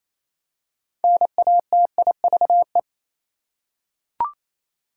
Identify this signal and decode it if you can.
Morse code, heard as a keyed tone, about 28 wpm, 700 Hz. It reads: NATIVE
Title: Six Letter Words - 28wpm